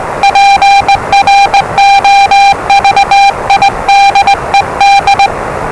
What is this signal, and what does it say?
Morse code, heard as a keyed tone, about 20 wpm, 800 Hz, with noise. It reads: PROVIDED
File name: WellingtonRadioZlwCloseDownOnHfAndMf / 01-Zlw 4277khz 417.5 qrt until 0925z